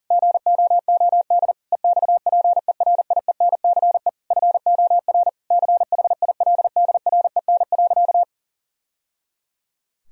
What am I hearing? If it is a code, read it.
Morse code, heard as a keyed tone, about 40 wpm, 700 Hz. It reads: GOOD EXPERIENCE FOR CHILDREN.
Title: Sentences from Top 1000 Words - Review Part 4 - 40wpm